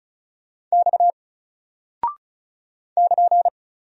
Morse code, heard as a keyed tone, about 36 wpm, 700 Hz. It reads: X <KN>
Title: Single Character (Letter:Number:Punctuation:Prosign) Alt 35wpm